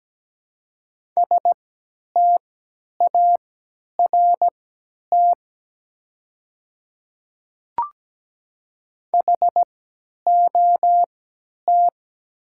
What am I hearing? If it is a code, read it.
Morse code, heard as a keyed tone, about 17 wpm, spaced as for 10 wpm, 700 Hz. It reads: START HOT
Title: 3x Character Spacing - Top 500 Words - 17wpm